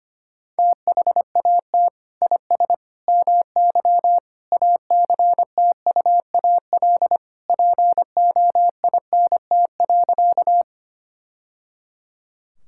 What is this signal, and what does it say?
Morse code, heard as a keyed tone, about 25 wpm, 700 Hz. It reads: THAT IS MY ACTUAL POINT.
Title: Sentences from Top 1000 Words - Review Part 5 - 25wpm